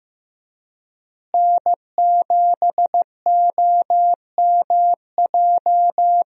Morse code, heard as a keyed tone, about 15 wpm, 700 Hz. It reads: N7OMJ
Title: USA Call Signs - 5 Characters - Repeated 2x - 15wpm